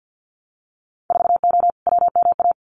Morse code, heard as a keyed tone, about 50 wpm, 700 Hz. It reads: 4O WMA
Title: Winter Field Day Exchanges Encore 50wpm